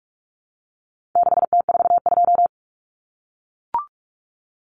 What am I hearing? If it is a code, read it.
Morse code, heard as a keyed tone, about 44 wpm, 700 Hz. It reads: 6T42
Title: Common Cut Numbers - 4-Digit 45wpm